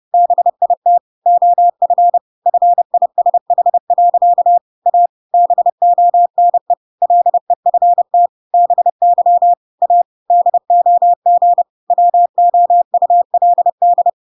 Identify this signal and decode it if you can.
Morse code, heard as a keyed tone, about 30 wpm, 700 Hz. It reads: BIT OF FISH. A BONE LEFT BY A DOG WOULD
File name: Pinocchio / Pinocchio ch5 30 30 ik1hge